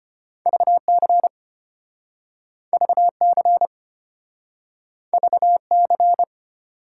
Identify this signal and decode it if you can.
Morse code, heard as a keyed tone, about 34 wpm, 700 Hz. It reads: VC VC VC